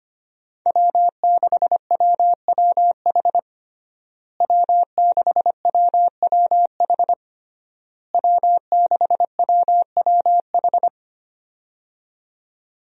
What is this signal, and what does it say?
Morse code, heard as a keyed tone, about 25 wpm, 700 Hz. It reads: W6WWH W6WWH W6WWH
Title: USA Call Signs - 5 Characters - Repeated 3x - 25wpm